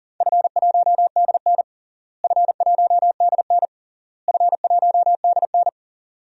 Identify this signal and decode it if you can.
Morse code, heard as a keyed tone, about 40 wpm, 700 Hz. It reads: F1DN F1DN F1DN
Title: DXCC-100 Call Signs - 4 Characters - Repeated 3x - 40wpm